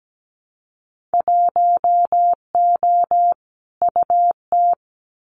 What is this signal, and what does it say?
Morse code, heard as a keyed tone, about 17 wpm, 700 Hz. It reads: 1O UT
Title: Winter Field Day Exchanges Encore 17wpm